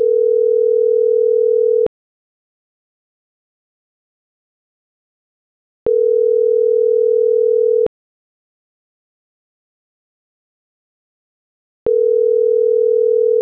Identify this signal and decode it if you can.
Ringback tone (North America), 440 + 480 Hz.